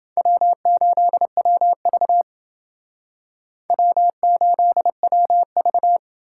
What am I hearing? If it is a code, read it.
Morse code, heard as a keyed tone, about 28 wpm, 700 Hz. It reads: W8WV W8WV